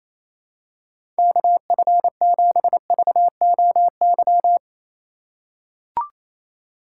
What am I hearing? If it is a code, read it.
Morse code, heard as a keyed tone, about 28 wpm, 700 Hz. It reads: KF7VOY
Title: USA Call Signs - 6 Characters - 3x Character Spacing - 28wpm